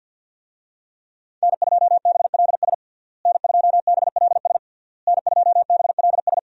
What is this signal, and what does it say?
Morse code, heard as a keyed tone, about 50 wpm, 700 Hz. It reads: N2BLS N2BLS N2BLS
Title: USA Call Signs - 5 Characters - Repeated 3x - 50wpm